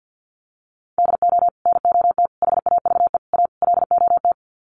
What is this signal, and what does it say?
Morse code, heard as a keyed tone, about 50 wpm, 700 Hz. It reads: DO NOT HAVE A LOT
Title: Sets of 5 Words Encore 50wpm